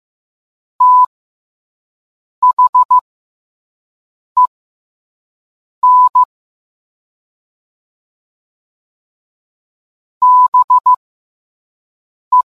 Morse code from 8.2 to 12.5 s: BE